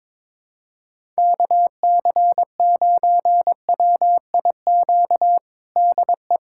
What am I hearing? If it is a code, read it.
Morse code, heard as a keyed tone, about 22 wpm, 700 Hz. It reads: KC9WIQ DE
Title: USA Callsign DE Callsign - 6 Characters - 22wpm